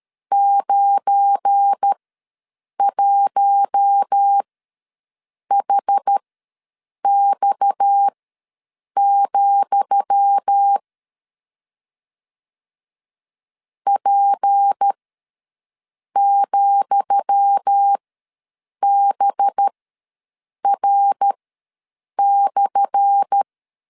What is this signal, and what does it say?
Morse code, heard as a keyed tone, about 13 wpm, spaced as for 7 wpm, 800 Hz. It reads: 91HX, P,BR/
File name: LEZIONI / 10 SIGNS / 10D COMA ws 12wpm 3618